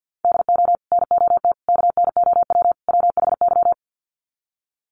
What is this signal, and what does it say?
Morse code, heard as a keyed tone, about 50 wpm, 700 Hz. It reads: DO NOT KNOW WHY
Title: Sets of 4 Words 50wpm